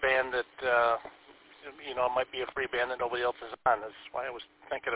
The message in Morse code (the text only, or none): none